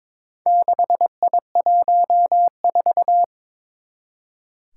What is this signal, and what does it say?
Morse code, heard as a keyed tone, about 23 wpm, 700 Hz. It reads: 6I14
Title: Four Challenging Letter-Number Combinations 22wpm